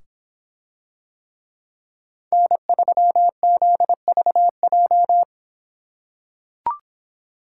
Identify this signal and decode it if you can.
Morse code, heard as a keyed tone, about 24 wpm, 700 Hz. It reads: N3ZVJ